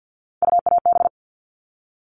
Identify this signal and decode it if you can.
Morse code, heard as a keyed tone, about 50 wpm, 700 Hz. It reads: UAB
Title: All Cut Numbers - 3-Digit 50wpm